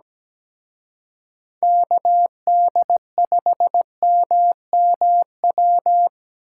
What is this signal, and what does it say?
Morse code, heard as a keyed tone, about 20 wpm, 700 Hz. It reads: KD5MMW
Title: Speed Racing - USA 2x3 Call Signs 17wpm